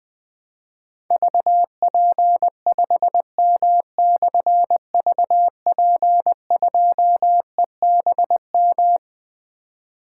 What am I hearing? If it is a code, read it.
Morse code, heard as a keyed tone, about 21 wpm, 700 Hz. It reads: VP5M/VP2EBM
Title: Difficult Call Signs 20wpm